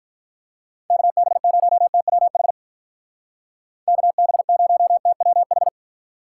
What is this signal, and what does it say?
Morse code, heard as a keyed tone, about 44 wpm, 700 Hz. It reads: KB0TWH KB0TWH